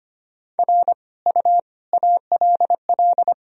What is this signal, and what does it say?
Morse code, heard as a keyed tone, about 25 wpm, 700 Hz. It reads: R U ALL
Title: Sentences from Top 400 Words and Abbreviations 25wpm